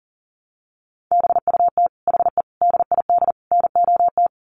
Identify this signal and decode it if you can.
Morse code, heard as a keyed tone, about 40 wpm, 700 Hz. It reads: BUT HE DID NOT
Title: Sets of 4 Words 40wpm